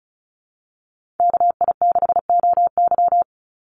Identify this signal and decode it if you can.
Morse code, heard as a keyed tone, about 34 wpm, 700 Hz. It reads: KI6OY
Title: Call Signs - 35wpm